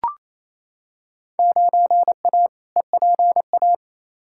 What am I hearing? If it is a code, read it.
Morse code, heard as a keyed tone, about 28 wpm, 700 Hz. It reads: 9A EPA
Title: Field Day Exchanges Encore - 28wpm